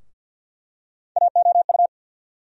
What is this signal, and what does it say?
Morse code, heard as a keyed tone, about 49 wpm, 690 Hz. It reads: AOU
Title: USA Call Sign Postfixes - 3 Letters 50wpm